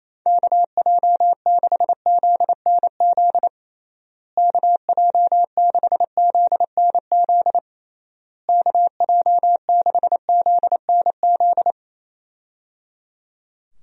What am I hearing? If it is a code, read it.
Morse code, heard as a keyed tone, about 27 wpm, 700 Hz. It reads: KJ6ZNZ KJ6ZNZ KJ6ZNZ
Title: USA Call Signs - 6 Characters - Repeated 3x - 28wpm